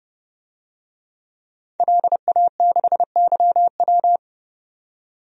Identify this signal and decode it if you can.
Morse code, heard as a keyed tone, about 32 wpm, 700 Hz. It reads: LA6YW
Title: DXCC-100 Call Signs - 5 Characters - 30wpm